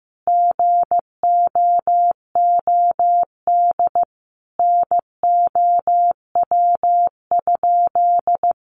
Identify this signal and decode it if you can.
Morse code, heard as a keyed tone, about 15 wpm, 700 Hz. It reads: GOOD NOW?
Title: Sentences from Top 200 Words 15wpm